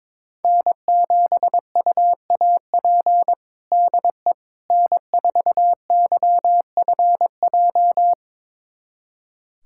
Morse code, heard as a keyed tone, about 22 wpm, 700 Hz. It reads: N7UAP DE N4YFJ
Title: USA Callsign DE Callsign 22wpm